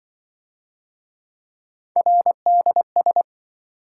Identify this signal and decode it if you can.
Morse code, heard as a keyed tone, about 26 wpm, 700 Hz. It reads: RDS